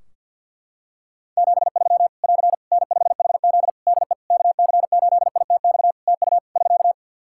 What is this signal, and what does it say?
Morse code, heard as a keyed tone, about 50 wpm, 700 Hz. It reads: 73 <AR> N5HZ DE KC8ITX TU <SK>
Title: Practice QSO - Exchange 4 - Ending - 50wpm